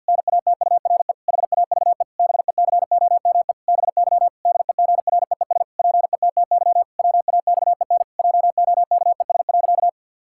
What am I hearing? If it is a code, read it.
Morse code, heard as a keyed tone, about 53 wpm, 700 Hz. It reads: NATURE HAVE BECOME BY DEGREES PETTY WAXEN JOKES.
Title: Pangram Challenge v2 - 50wpm